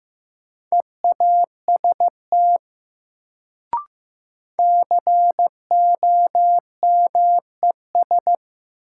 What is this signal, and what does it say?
Morse code, heard as a keyed tone, about 15 wpm, 700 Hz. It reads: EAST COMES
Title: Sets of 3 Words - New Word Drill - 15wpm